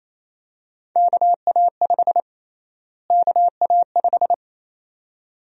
Morse code, heard as a keyed tone, about 28 wpm, 700 Hz. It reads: KA5 KA5
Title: USA Call Sign Prefixes - Repeated 2x - 28wpm